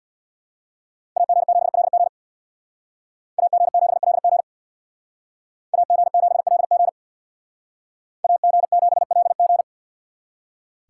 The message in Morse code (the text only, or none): AG7LG AG7LG AG7LG AG7LG